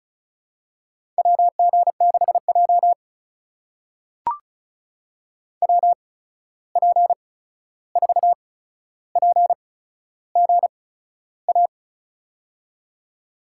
Morse code, heard as a keyed tone, about 34 wpm, spaced as for 13 wpm, 700 Hz. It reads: WG6J WP4PGA
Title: USA Call Signs - 8x Character Spacing - 35wpm